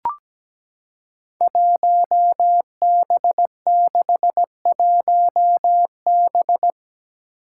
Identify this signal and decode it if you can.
Morse code, heard as a keyed tone, about 17 wpm, 700 Hz. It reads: 1B61B
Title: Five Challenging Letter-Number Combinations 17wpm